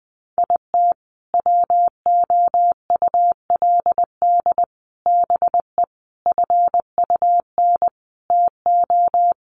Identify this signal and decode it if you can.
Morse code, heard as a keyed tone, about 20 wpm, 700 Hz. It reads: IT WOULD BE FUN TO
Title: Sentences from Top 1300 Words 20wpm